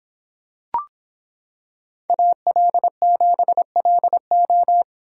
Morse code, heard as a keyed tone, about 26 wpm, 700 Hz. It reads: AL7LO